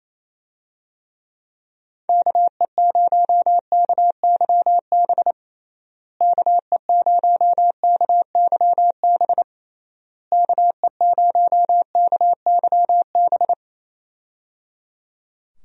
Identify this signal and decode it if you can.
Morse code, heard as a keyed tone, about 28 wpm, 700 Hz. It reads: KE0KYB KE0KYB KE0KYB
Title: USA Call Signs - 6 Characters - Repeated 3x - 28wpm